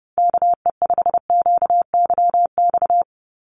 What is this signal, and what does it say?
Morse code, heard as a keyed tone, about 30 wpm, 700 Hz. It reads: KE5QYX
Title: USA Call Signs 30wpm